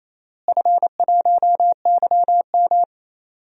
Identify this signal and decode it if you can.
Morse code, heard as a keyed tone, about 28 wpm, 700 Hz. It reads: F1YM
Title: DXCC-100 Call Signs - 4 Characters - Repeated 3x - 28wpm